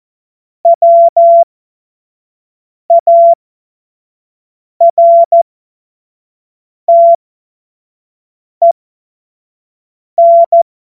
Morse code, heard as a keyed tone, about 14 wpm, spaced as for 5 wpm, 680 Hz. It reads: WARTEN
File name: Morsekurs / Witze / Witz 20 Mai 2025 25 F70